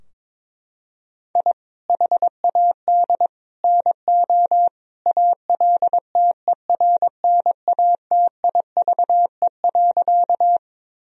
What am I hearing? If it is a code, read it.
Morse code, heard as a keyed tone, about 22 wpm, 700 Hz. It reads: I HAD NO ALTERNATIVE.
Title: Sentences from Top 1000 Words - Review Part 4 - 22wpm